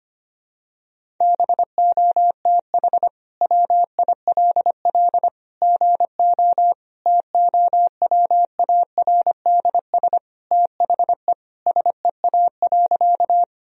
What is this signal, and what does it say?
Morse code, heard as a keyed tone, about 25 wpm, 700 Hz. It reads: BOTH WILL GO TOWARDS THE SEA.
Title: Sentences from Top 1000 Words - Review Part 2 - 25wpm